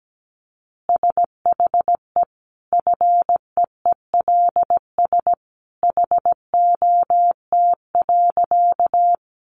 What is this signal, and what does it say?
Morse code, heard as a keyed tone, about 17 wpm, 700 Hz. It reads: SHE FEELS HOT.
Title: Easy Phrases 17wpm